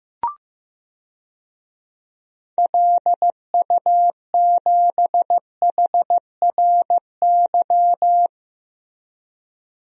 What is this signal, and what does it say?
Morse code, heard as a keyed tone, about 15 wpm, 700 Hz. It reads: LU7HRY
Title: DXCC-100 Call Signs - 6 Characters - Repeated 3x - 15wpm